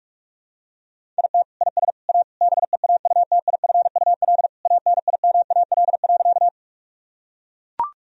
Morse code, heard as a keyed tone, about 43 wpm, 700 Hz. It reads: IT IS A BEAUTIFUL ANIMAL.
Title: Sentences from Top 1000 Words - Review Part 1 - 45wpm